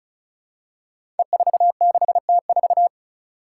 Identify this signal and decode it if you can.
Morse code, heard as a keyed tone, about 36 wpm, 700 Hz. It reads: E46T4
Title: Five Challenging Letter-Number Combinations 35wpm